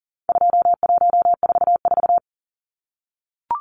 Morse code, heard as a keyed tone, about 40 wpm, 700 Hz. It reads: 2144
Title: Numbers - 4-Digit 40wpm